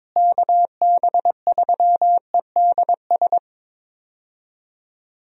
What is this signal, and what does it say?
Morse code, heard as a keyed tone, about 22 wpm, 700 Hz. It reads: KB3EDS
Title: USA Call Signs - 6 Characters - Repeated 3x - 22wpm